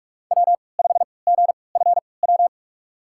Morse code, heard as a keyed tone, about 42 wpm, spaced as for 25 wpm, 700 Hz. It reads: W5GFW
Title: USA Call Signs - 5 Characters - 3x Character Spacing - 45wpm